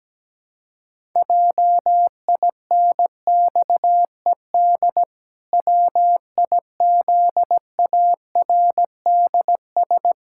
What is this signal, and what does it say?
Morse code, heard as a keyed tone, about 17 wpm, 700 Hz. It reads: JINXED WIZARDS
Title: Pangram Challenge v2 - 17wpm